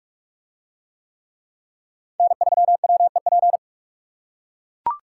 Morse code, heard as a keyed tone, about 44 wpm, 700 Hz. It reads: N3WEP